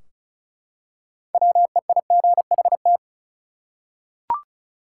Morse code, heard as a keyed tone, about 36 wpm, 700 Hz. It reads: WEIGHT
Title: Six Letter Words - 35wpm